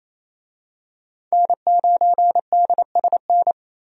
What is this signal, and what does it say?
Morse code, heard as a keyed tone, about 28 wpm, 700 Hz. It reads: N9DSN
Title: USA Call Signs - 5 Characters - 28wpm